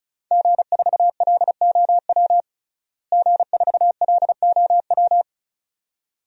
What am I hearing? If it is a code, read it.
Morse code, heard as a keyed tone, about 36 wpm, 700 Hz. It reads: G4LOW G4LOW
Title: DXCC-100 Call Signs - 5 Characters - Repeated 2x - 35wpm